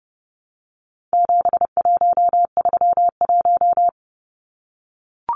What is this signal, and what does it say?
Morse code, heard as a keyed tone, about 30 wpm, 700 Hz. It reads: 7131
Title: Numbers - 4-Digit 30wpm